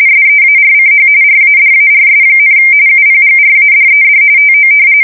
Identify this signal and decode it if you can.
RTTY, about 45.45 baud. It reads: SEARCH RADARS, UPGRADED TO
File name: fsk170 / battleshipnj